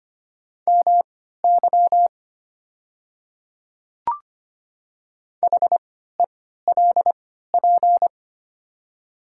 Morse code, heard as a keyed tone, about 25 wpm, spaced as for 14 wpm, 690 Hz. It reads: MY HELP
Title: 3x Character Spacing - Top 300 Words - 25wpm